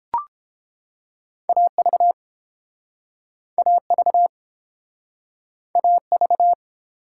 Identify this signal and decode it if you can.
Morse code, heard as a keyed tone, about 31 wpm, 700 Hz. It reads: AV AV AV